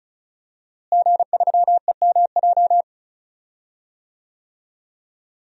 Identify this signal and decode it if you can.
Morse code, heard as a keyed tone, about 36 wpm, 700 Hz. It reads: G3EMJ
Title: DXCC-100 Call Signs - 5 Characters - Repeated 3x - 35wpm